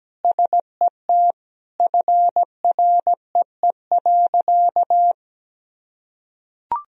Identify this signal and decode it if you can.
Morse code, heard as a keyed tone, about 17 wpm, 700 Hz. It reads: SET FREE.
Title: Sentences from Top 1000 Words - Review Part 1 - 17wpm